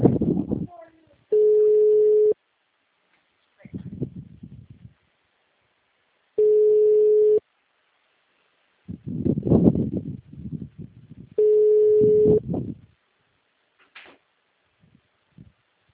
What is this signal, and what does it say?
Ringback tone (Europe, 425 Hz), with speech.
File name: out- -2032- - - .3774